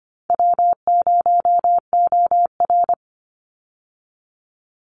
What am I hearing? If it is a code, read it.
Morse code, heard as a keyed tone, about 25 wpm, 700 Hz. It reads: W0OR
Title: Easy Call Signs Repeated 3x - 25wpm